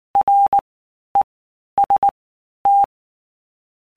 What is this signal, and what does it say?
Morse code, heard as a keyed tone, about 19 wpm, 800 Hz. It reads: R E S T
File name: Basic Letter Groups By Blocks / Groups@18 / LFCYQ@18